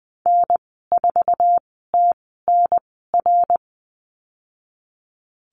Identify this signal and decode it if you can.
Morse code, heard as a keyed tone, about 19 wpm, spaced as for 14 wpm, 700 Hz. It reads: N4TNR
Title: 088 Lesson 15g - Number 5 - Call Signs 20wpm